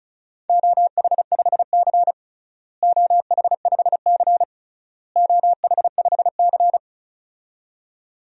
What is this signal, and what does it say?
Morse code, heard as a keyed tone, about 36 wpm, 690 Hz. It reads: OH5C OH5C OH5C
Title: DXCC-100 Call Signs - 4 Characters - Repeated 3x - 35wpm